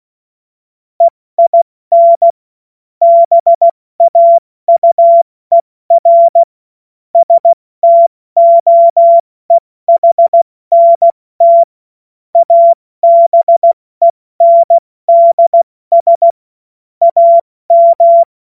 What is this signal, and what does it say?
Morse code, heard as a keyed tone, about 16 wpm, spaced as for 14 wpm, 680 Hz. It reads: EIN BAUER STOEHNT ABENDS AM
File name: Morsekurs / Witze / Witz 18. Mai 2021, 70 F80